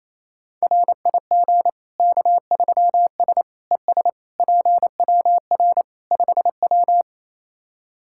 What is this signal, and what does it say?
Morse code, heard as a keyed tone, about 28 wpm, 700 Hz. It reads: RIG K3S ES PWR 5W
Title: K1USN SST QSO Practice 28wpm